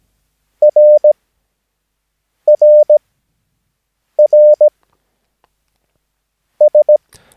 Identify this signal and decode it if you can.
Morse code, heard as a keyed tone, about 17 wpm, spaced as for 5 wpm, 600 Hz. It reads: RRRS
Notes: Övningsmaterial med alfabetet A till Ö samt siffror 0 till 9 enligt International/Continental Morse Code.